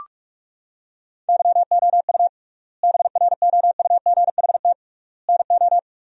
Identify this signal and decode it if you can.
Morse code, heard as a keyed tone, about 46 wpm, 700 Hz. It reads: YOU BROUGHT NO
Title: 3x Word-Spacing - Sentences from Top 500 Words - 45wpm